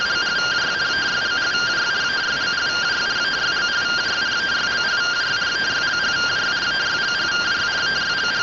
Two-tone teleprinter signal, about 170 Hz shift.
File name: findings / CT 1 Scout 0002.gen-13 rtty rx